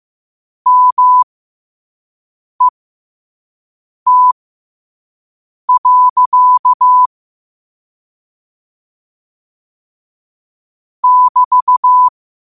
Morse code, heard as a keyed tone, about 15 wpm, spaced as for 5 wpm, 1000 Hz. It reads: MET. <BT>
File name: 5wpm